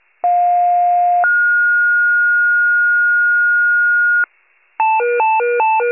Two-tone page, heard as a one-second tone, then a three-second tone, about 710 then 1430 Hz. It is heard from 0.2 to 4.2 s.
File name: pager / 46 FIRE / wcalert